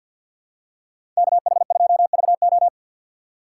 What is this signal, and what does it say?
Morse code, heard as a keyed tone, about 50 wpm, 700 Hz. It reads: KH2VO
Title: USA Call Signs - 5 Characters - 2x Character Spacing - 50wpm